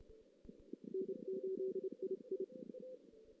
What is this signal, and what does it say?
Morse code, heard as a keyed tone, about 27 wpm, 380 Hz. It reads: N8II
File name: 2016 JAN CW NAQP / 2016 Jan naqp k5ot - 0-3